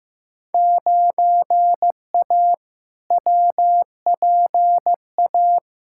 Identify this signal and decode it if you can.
Morse code, heard as a keyed tone, about 15 wpm, 700 Hz. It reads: 9A WPA
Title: Field Day Exchanges Encore - 15wpm